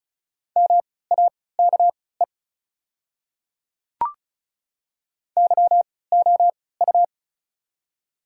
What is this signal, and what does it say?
Morse code, heard as a keyed tone, about 36 wpm, spaced as for 20 wpm, 700 Hz. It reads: MAKE YOU
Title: 3x Character Spacing - Top 100 Words - 35wpm